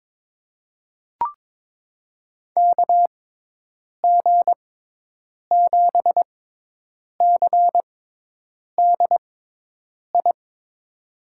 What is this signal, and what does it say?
Morse code, heard as a keyed tone, about 22 wpm, spaced as for 8 wpm, 700 Hz. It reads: KG7CDI
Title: USA Call Signs - 6 Characters - 6x Character Spacing - 22wpm